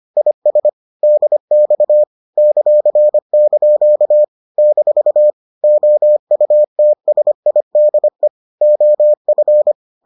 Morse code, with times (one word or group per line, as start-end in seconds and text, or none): 0.2-0.7 IS
1.0-2.0 DX
2.4-4.2 ;)
4.6-5.3 -
5.6-8.3 OUTSIDE
8.6-9.7 OF